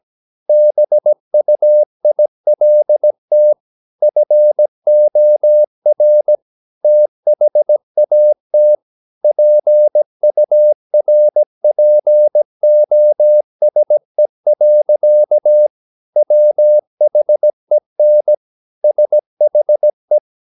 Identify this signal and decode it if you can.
Morse code, heard as a keyed tone, about 17 wpm, 600 Hz. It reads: BUILT FOR THAT PURPOSE. WHEN SHE
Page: Romanen Never af Ken Follet på engelsk.